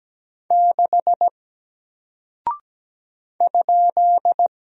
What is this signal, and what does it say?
Morse code, heard as a keyed tone, about 17 wpm, 700 Hz. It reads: 6 ?
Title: Single Character (Letter:Number:Punctuation:Prosign) Alt 17wpm